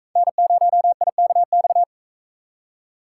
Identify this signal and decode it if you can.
Morse code, heard as a keyed tone, about 40 wpm, 700 Hz. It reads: N0IKX